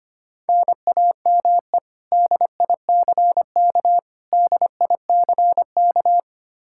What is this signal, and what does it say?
Morse code, heard as a keyed tone, about 25 wpm, 700 Hz. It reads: NAME DICK DICK
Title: Practice QSO - Exchange 1 - Beginning - 25wpm